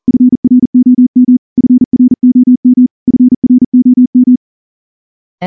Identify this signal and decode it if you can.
Morse code, heard as a keyed tone, about 38 wpm, 250 Hz. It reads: FROM FROM FROM
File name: morse-training-40WPM